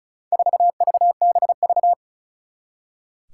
Morse code, heard as a keyed tone, about 35 wpm, 700 Hz. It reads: 4VBV
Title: Four Challenging Letter-Number Combinations 35wpm